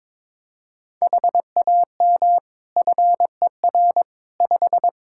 Morse code, heard as a keyed tone, about 22 wpm, 700 Hz. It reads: HAM FER 5
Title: Practice QSO - Exchange 3 - More Info - 22wpm